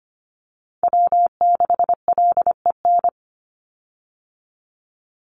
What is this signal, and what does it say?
Morse code, heard as a keyed tone, about 25 wpm, 700 Hz. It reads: W6LEN
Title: Call Signs Repeated 3x - 25wpm